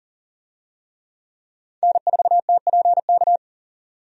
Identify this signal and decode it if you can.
Morse code, heard as a keyed tone, about 40 wpm, 700 Hz. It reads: N4TPK
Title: USA Call Signs - 5 Characters - 40wpm